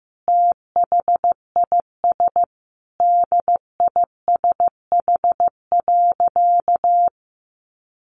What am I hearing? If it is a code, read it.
Morse code, heard as a keyed tone, about 15 wpm, 700 Hz. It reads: THIS DISH.
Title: Sentences from Top 1300 Words 15wpm